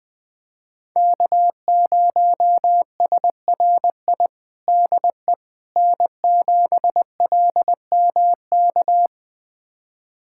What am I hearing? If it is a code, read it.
Morse code, heard as a keyed tone, about 20 wpm, 700 Hz. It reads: K0SRI DE N7LMK
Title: USA Callsign DE Callsign - 5 Characters - 20wpm